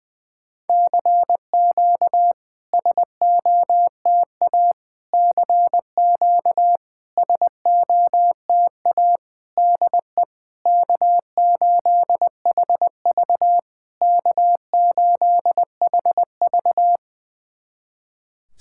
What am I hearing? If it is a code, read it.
Morse code, heard as a keyed tone, about 20 wpm, 690 Hz. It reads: CQ SOTA CQ SOTA DE K8HV K8HV
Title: SOTA QSO Practice 20wpm